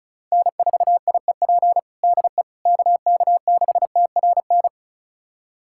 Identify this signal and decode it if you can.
Morse code, heard as a keyed tone, about 36 wpm, 700 Hz. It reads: N4IEP DE KK6TRN
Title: USA Callsign DE Callsign 35wpm